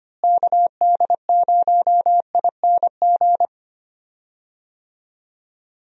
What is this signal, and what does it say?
Morse code, heard as a keyed tone, about 25 wpm, 700 Hz. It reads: KD0ING
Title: USA Call Signs - 6 Characters - Repeated 2x - 25wpm